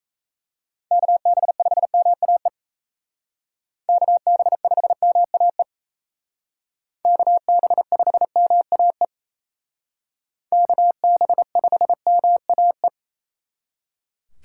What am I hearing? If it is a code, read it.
Morse code, heard as a keyed tone, about 33 wpm, 700 Hz. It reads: KB5MAE KB5MAE KB5MAE KB5MAE
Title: Speed Racing - USA 2x3 Call Signs 28wpm